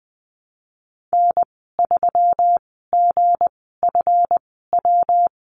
Morse code, heard as a keyed tone, about 20 wpm, spaced as for 15 wpm, 700 Hz. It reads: N3GFW
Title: 161 Lesson 26g - Letter g - Call Signs 20wpm